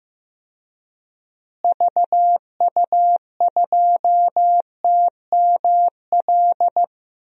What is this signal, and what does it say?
Morse code, heard as a keyed tone, about 15 wpm, 700 Hz. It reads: VU2TML
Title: DXCC-100 Call Signs - 6 Characters - Repeated 2x - 15wpm